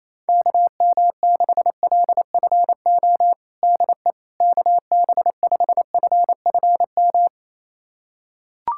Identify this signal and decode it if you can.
Morse code, heard as a keyed tone, about 28 wpm, 700 Hz. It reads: KM6LFO DE KB5FFM
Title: USA Callsign DE Callsign - 6 Characters - 28wpm